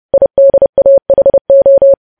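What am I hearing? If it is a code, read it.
Morse code, heard as a keyed tone, about 30 wpm, 550 Hz. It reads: IDAHO
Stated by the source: News headlines sent in Morse at 30 wpm, preceded by VVV.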